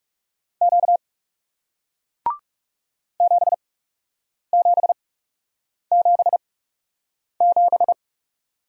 Morse code, heard as a keyed tone, about 40 wpm, spaced as for 8 wpm, 700 Hz. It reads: Q 7777